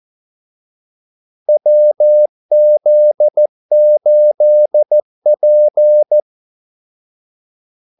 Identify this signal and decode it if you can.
Morse code, heard as a keyed tone, about 14 wpm, 600 Hz. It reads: WZ8P
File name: Callsigns-14wpm